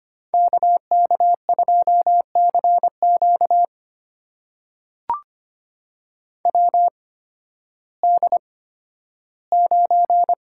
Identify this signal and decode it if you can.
Morse code, heard as a keyed tone, about 25 wpm, spaced as for 13 wpm, 700 Hz. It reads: KK2CQ WD9
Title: USA Call Signs - 8x Character Spacing - 25wpm